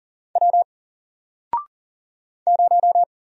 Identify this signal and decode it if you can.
Morse code, heard as a keyed tone, about 40 wpm, 700 Hz. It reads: W 0
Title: Single Character (Letter:Number:Punctuation:Prosign) Alt 40wpm